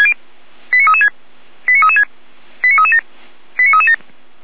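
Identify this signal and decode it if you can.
Five-tone selective call (ZVEI), heard as quick runs of five tones, tones of about 80 ms. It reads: 89336 89336 89337 89337
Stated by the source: ZVEI five-tone selective call received on an SDR.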